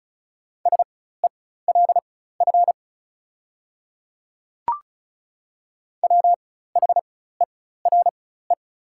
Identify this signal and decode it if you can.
Morse code, heard as a keyed tone, about 34 wpm, spaced as for 16 wpm, 690 Hz. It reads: SELF WHERE
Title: 4x Character Spacing - Top 300 Words - 35wpm